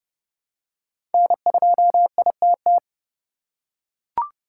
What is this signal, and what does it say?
Morse code, heard as a keyed tone, about 30 wpm, 700 Hz. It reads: N2ITT
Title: USA Call Signs - 5 Characters - 2x Character Spacing - 30wpm